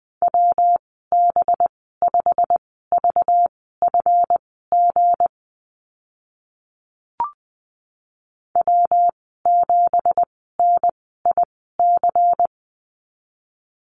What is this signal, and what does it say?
Morse code, heard as a keyed tone, about 20 wpm, spaced as for 14 wpm, 700 Hz. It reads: WB5VFG W7NIC
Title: 193 Lesson 31g - Letter v - Call Signs 20wpm